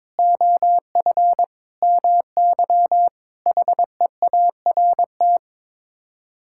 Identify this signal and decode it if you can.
Morse code, heard as a keyed tone, about 22 wpm, 700 Hz. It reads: OF MY HEART
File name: practice / 22wpm / Sets of 6 Words - 22wpm